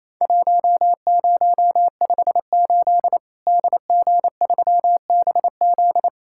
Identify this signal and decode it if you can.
Morse code, heard as a keyed tone, about 27 wpm, 700 Hz. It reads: 1058 DG3BZ
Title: Worked All Europe DX Contest - QTC Exchange - 28wpm